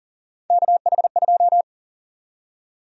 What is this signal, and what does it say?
Morse code, heard as a keyed tone, about 40 wpm, 700 Hz. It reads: KH2
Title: USA Call Sign Prefixes - 40wpm